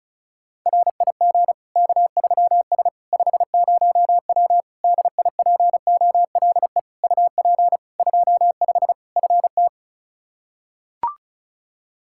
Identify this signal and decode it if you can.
Morse code, heard as a keyed tone, about 34 wpm, 700 Hz. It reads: RIG K3S 50W DIPOLE UP 25 FT
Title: Practice QSO - Exchange 2 - Info - 35wpm